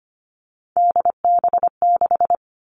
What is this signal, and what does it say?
Morse code, heard as a keyed tone, about 25 wpm, 700 Hz. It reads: DB6
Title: Three Challenging Letter-Number Combinations 25wpm